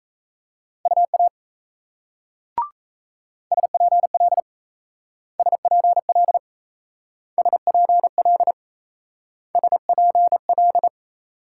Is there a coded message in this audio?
Yes: Morse code from 0.9 to 10.9 s, heard as a keyed tone, about 38 wpm, 700 Hz. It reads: UA SPL SPL SPL SPL